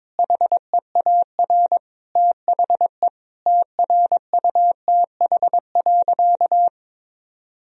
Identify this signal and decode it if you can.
Morse code, heard as a keyed tone, about 22 wpm, 700 Hz. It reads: HEAR THE TRUTH.
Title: Sentences from Top 1000 Words - Review Part 1 - 22wpm